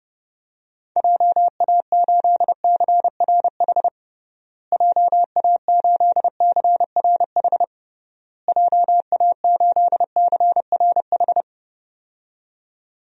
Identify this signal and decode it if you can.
Morse code, heard as a keyed tone, about 30 wpm, 700 Hz. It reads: JA8CRH JA8CRH JA8CRH
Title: DXCC-100 Call Signs - 6 Characters - Repeated 3x - 30wpm